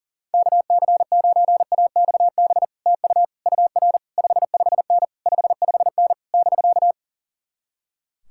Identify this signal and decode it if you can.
Morse code, heard as a keyed tone, about 49 wpm, 700 Hz. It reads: KC9AXB TU UR 55N 55N <BK>
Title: Speed-Racing Common QSO Phrases - 40wpm